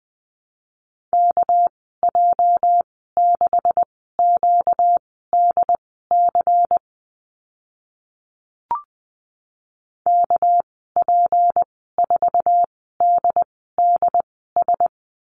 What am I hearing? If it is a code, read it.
Morse code, heard as a keyed tone, about 20 wpm, spaced as for 14 wpm, 700 Hz. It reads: KJ6QDC KP4DDS
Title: 245 Lesson 39g - Letter z - Call Signs 20wpm